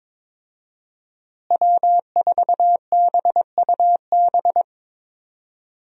Morse code, heard as a keyed tone, about 23 wpm, 700 Hz. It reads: W4BUB
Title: USA Call Signs - 5 Characters - Repeated 3x - 22wpm